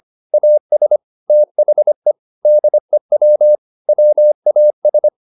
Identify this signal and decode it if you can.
Morse code, heard as a keyed tone, about 25 wpm, 600 Hz. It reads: AS THE DEW WAS
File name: the old man and the sea part 1 25wpm